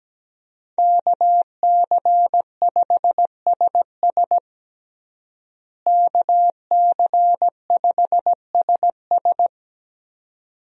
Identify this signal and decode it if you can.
Morse code, heard as a keyed tone, about 17 wpm, 700 Hz. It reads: KC5SS KC5SS
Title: USA Call Signs - 5 Characters - Repeated 2x - 17wpm